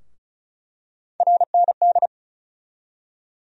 Morse code, heard as a keyed tone, about 34 wpm, 700 Hz. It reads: RND
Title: USA Call Sign Postfixes - 3 Letters 35wpm